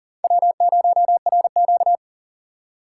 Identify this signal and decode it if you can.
Morse code, heard as a keyed tone, about 40 wpm, 700 Hz. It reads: W0RQ
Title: USA Call Signs - 4 Characters - 2x Character Spacing - 40wpm